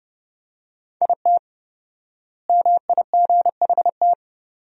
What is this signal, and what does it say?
Morse code, heard as a keyed tone, about 30 wpm, 700 Hz. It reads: IT MIGHT
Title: 4x Word-Spacing - Sentences from Top 300 Words - 30wpm